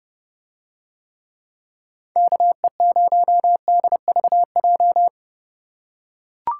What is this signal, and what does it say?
Morse code, heard as a keyed tone, about 28 wpm, 700 Hz. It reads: KE0DVJ